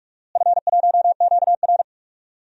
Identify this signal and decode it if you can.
Morse code, heard as a keyed tone, about 46 wpm, 700 Hz. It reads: F1QR
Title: DXCC-100 Call Signs - 4 Characters - Repeated 2x - 45wpm